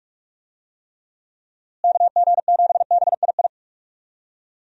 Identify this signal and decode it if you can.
Morse code, heard as a keyed tone, about 46 wpm, 700 Hz. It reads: KG7BII
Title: USA Call Signs - 6 Characters - Repeated 2x - 45wpm